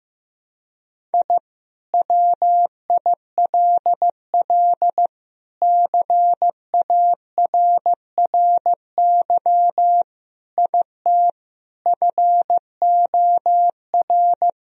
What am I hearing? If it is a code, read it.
Morse code, heard as a keyed tone, about 16 wpm, 700 Hz. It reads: I WILL CARRY IT FOR
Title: Sentences from Top 1000 Words - Review Part 1 - 15wpm